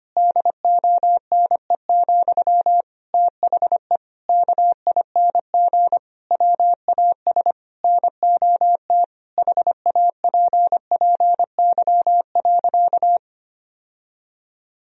Morse code, heard as a keyed tone, about 25 wpm, 700 Hz. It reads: DONE, THE KING WAS NOT HAPPY.
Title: Sentences from Top 1000 Words - Review Part 3 - 25wpm